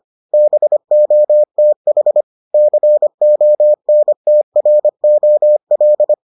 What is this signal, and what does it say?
Morse code, heard as a keyed tone, about 25 wpm, 600 Hz. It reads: BOTH CONTROL